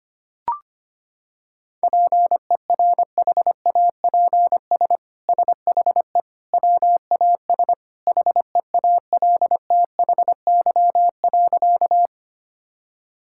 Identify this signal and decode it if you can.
Morse code, heard as a keyed tone, about 25 wpm, 700 Hz. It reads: PERHAPS SHE WAS HEALTHY.
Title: Sentences from Top 1000 Words - Review Part 5 - 25wpm